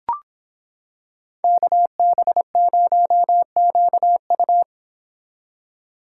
Morse code, heard as a keyed tone, about 26 wpm, 700 Hz. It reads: KB0QU